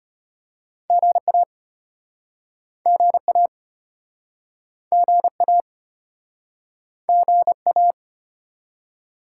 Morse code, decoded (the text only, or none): GA GA GA GA